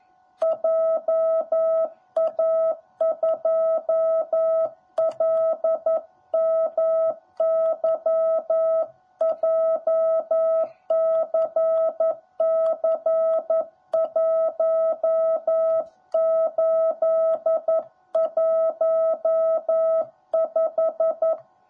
Morse code, heard as a keyed tone, about 11 wpm, 650 Hz. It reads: JA2LMY JCC1815
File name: jh8cbh / 2021cw / cw10b